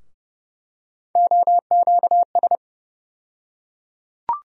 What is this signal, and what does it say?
Morse code, heard as a keyed tone, about 31 wpm, 700 Hz. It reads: OQS
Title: USA Call Sign Postfixes - 3 Letters 30wpm